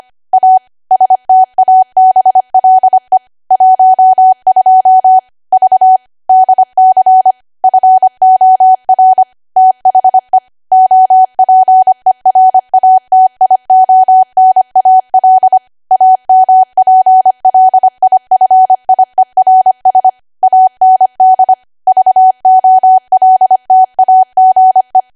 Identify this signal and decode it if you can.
Morse code, heard as a keyed tone, about 25 wpm, 750 Hz. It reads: A STABLE 12 V DC FOR THE OPERATIONAL AMPLIFIERS AND VOLTAGE
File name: Archive / 25 WPM / 25WPM